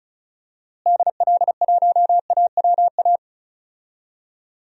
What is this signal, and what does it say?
Morse code, heard as a keyed tone, about 36 wpm, 700 Hz. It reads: DL1AWA
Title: DXCC-100 Call Signs - 6 Characters - Repeated 3x - 35wpm